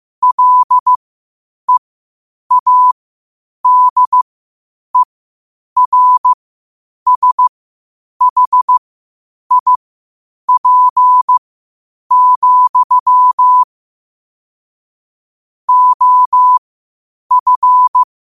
Morse code, heard as a keyed tone, about 15 wpm, spaced as for 8 wpm, 1000 Hz. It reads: LEADERSHIP, OF
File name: qotdfiles / 8wpm